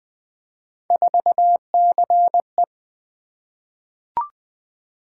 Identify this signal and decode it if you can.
Morse code, heard as a keyed tone, about 20 wpm, 700 Hz. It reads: 4CE
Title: Three Letter-Number Combinations - 20wpm